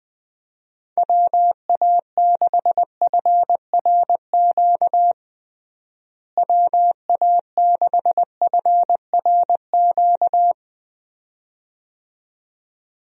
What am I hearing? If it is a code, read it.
Morse code, heard as a keyed tone, about 21 wpm, 700 Hz. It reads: WA6FRQ WA6FRQ
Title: USA Call Signs - 6 Characters - Repeated 2x - 20wpm